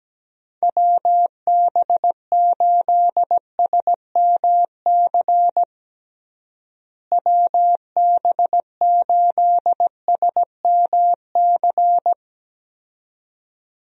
Morse code, heard as a keyed tone, about 17 wpm, 700 Hz. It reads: WB8SMC WB8SMC
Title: USA Call Signs - 6 Characters - Repeated 2x - 17wpm